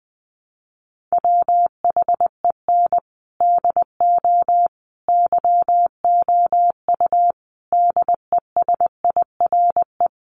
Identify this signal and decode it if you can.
Morse code, heard as a keyed tone, about 19 wpm, 700 Hz. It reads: WHEN DO YOU DESIRE
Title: Sentences from Top 1100 Words 20wpm